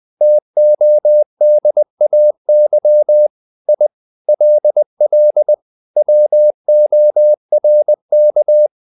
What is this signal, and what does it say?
Morse code, heard as a keyed tone, about 20 wpm, 600 Hz. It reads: TODAY I LL WORK
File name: the old man and the sea / the old man and the sea part 1 20wpm